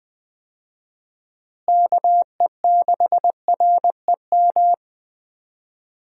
Morse code, heard as a keyed tone, about 21 wpm, 700 Hz. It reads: KE6REM
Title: USA Call Signs - 6 Characters - Repeated 2x - 20wpm